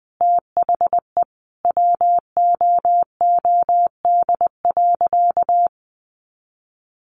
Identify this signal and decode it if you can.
Morse code, heard as a keyed tone, about 20 wpm, 700 Hz. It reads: THE WOOD.
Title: Sentences from Top 1100 Words 20wpm